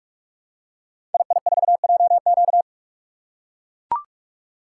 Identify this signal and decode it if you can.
Morse code, heard as a keyed tone, about 44 wpm, 700 Hz. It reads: II4JQ
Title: Code Groups - 5 Letter:Numbers - 45wpm